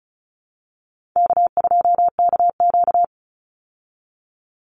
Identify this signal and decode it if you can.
Morse code, heard as a keyed tone, about 34 wpm, 700 Hz. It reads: K2KQ
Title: Call Signs Repeated 3x - 35wpm